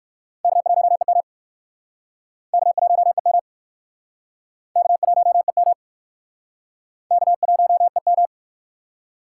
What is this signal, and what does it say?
Morse code, heard as a keyed tone, about 55 wpm, 700 Hz. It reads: K1EM K1EM K1EM K1EM
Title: Speed Racing - USA 1x2 Call Signs 45wpm